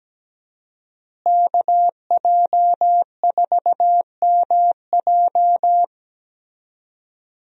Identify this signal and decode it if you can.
Morse code, heard as a keyed tone, about 17 wpm, 700 Hz. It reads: KJ4MJ
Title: USA Call Signs - 5 Characters - Repeated 2x - 17wpm